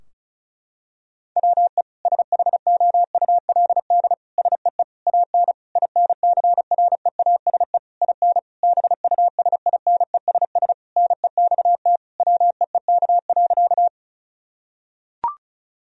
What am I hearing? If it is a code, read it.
Morse code, heard as a keyed tone, about 36 wpm, 700 Hz. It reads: WE SHOULD SEE AN INCREASE IN BUSINESS NEXT WEEK.
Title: Sentences from Top 1000 Words - Review Part 4 - 35wpm